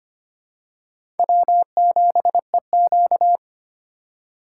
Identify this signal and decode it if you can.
Morse code, heard as a keyed tone, about 25 wpm, 700 Hz. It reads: W7EQ
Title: USA Call Signs - 4 Characters - 3x Character Spacing - 25wpm